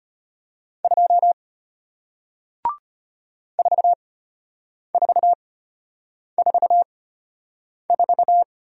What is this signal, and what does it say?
Morse code, heard as a keyed tone, about 33 wpm, spaced as for 8 wpm, 700 Hz. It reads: 2 4444